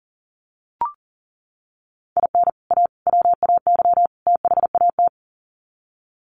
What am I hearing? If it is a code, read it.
Morse code, heard as a keyed tone, about 40 wpm, 700 Hz. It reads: IN A WAY THAT
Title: Sets of 4 Words 40wpm